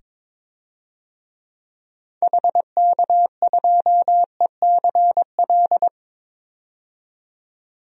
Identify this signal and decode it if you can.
Morse code, heard as a keyed tone, about 21 wpm, 700 Hz. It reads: HK2ECL
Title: DXCC-100 Call Signs - 6 Characters - 22wpm